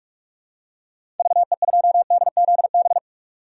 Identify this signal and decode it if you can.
Morse code, heard as a keyed tone, about 46 wpm, 700 Hz. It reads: VE2DZB
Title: DXCC-100 Call Signs - 6 Characters - 45wpm